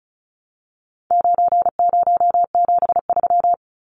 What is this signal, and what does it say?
Morse code, heard as a keyed tone, about 34 wpm, 700 Hz. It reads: 9073
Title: Numbers - 4-Digit 35wpm